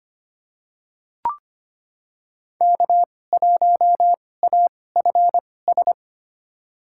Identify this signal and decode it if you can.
Morse code, heard as a keyed tone, about 25 wpm, spaced as for 18 wpm, 700 Hz. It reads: K1AFS
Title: USA Call Signs - 5 Characters - 2x Character Spacing - 25wpm